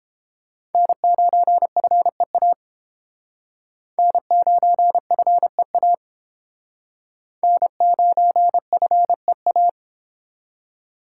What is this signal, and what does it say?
Morse code, heard as a keyed tone, about 30 wpm, 700 Hz. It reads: N9FEA N9FEA N9FEA